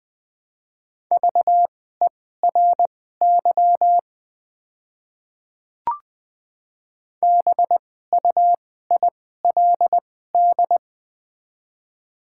Morse code, heard as a keyed tone, about 20 wpm, spaced as for 14 wpm, 700 Hz. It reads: VERY BUILD
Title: 2x Character Spacing - Top 200 Words - 20wpm